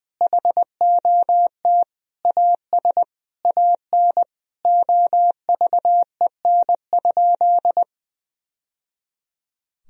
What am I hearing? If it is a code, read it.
Morse code, heard as a keyed tone, about 19 wpm, 700 Hz. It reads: HOT AS AN OVEN?
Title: Sentences from Top 1000 Words - Review Part 4 - 20wpm